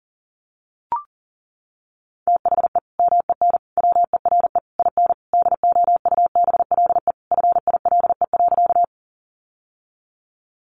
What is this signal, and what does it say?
Morse code, heard as a keyed tone, about 40 wpm, 700 Hz. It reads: THE MEN WERE IN DOUBLE FILE.
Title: Sentences from Top 1400 Words 40wpm